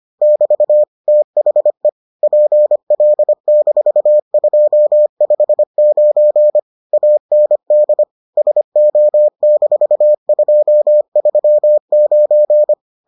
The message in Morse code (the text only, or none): <BT> THE PL-259 AND SO-239